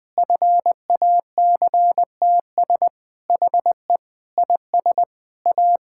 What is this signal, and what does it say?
Morse code, heard as a keyed tone, about 21 wpm, 700 Hz. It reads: FACTS HE IS A
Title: Sentences with 3-Word Phrases - 20wpm